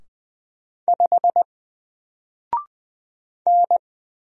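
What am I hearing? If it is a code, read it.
Morse code, heard as a keyed tone, about 20 wpm, 700 Hz. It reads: 5 N
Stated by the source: Single letter and number Morse drill at 20 wpm, from N4LCD.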